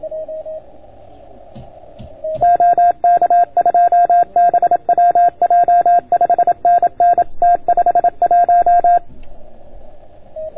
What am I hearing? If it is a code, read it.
Morse code, heard as a keyed tone, about 28 wpm, 690 Hz. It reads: OK2BWJ 5NN T51